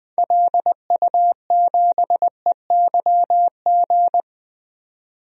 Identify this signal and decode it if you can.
Morse code, heard as a keyed tone, about 20 wpm, 700 Hz. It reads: LU7EYG
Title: DXCC-100 Call Signs - 6 Characters - 20wpm